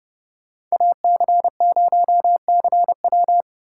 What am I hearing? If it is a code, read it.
Morse code, heard as a keyed tone, about 33 wpm, 700 Hz. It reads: AC0CW